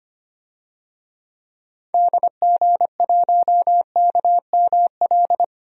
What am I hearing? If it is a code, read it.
Morse code, heard as a keyed tone, about 25 wpm, 700 Hz. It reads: DG1KML
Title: DXCC-100 Call Signs - 6 Characters - 25wpm